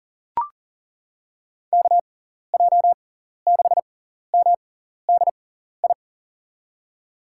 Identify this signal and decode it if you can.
Morse code, heard as a keyed tone, about 37 wpm, spaced as for 14 wpm, 700 Hz. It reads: KJ6MDI
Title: USA Call Signs - 6 Characters - 6x Character Spacing - 40wpm